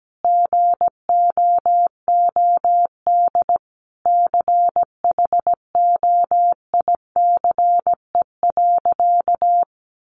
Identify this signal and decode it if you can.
Morse code, heard as a keyed tone, about 17 wpm, 700 Hz. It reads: GOOD CHOICE.
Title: Sentences from Top 900 Words 17wpm